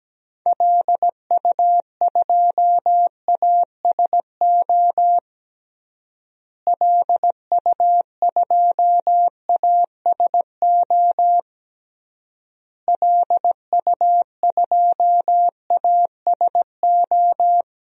Morse code, heard as a keyed tone, about 17 wpm, 700 Hz. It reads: LU2ASO LU2ASO LU2ASO
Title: DXCC-100 Call Signs - 6 Characters - Repeated 3x - 17wpm